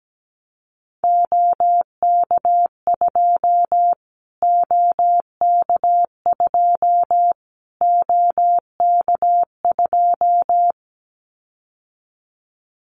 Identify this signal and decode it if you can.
Morse code, heard as a keyed tone, about 17 wpm, 700 Hz. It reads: OK2 OK2 OK2
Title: Callsign Prefixes for Top 25 DXCC - Repeated 3x - 17wpm